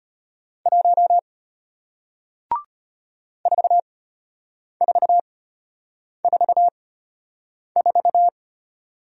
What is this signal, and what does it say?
Morse code, heard as a keyed tone, about 33 wpm, spaced as for 8 wpm, 700 Hz. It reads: 1 4444